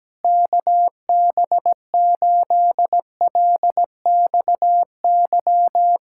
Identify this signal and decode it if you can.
Morse code, heard as a keyed tone, about 17 wpm, 700 Hz. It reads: KB8LXY
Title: Speed-Racing Common QSO Phrases Encore - 17wpm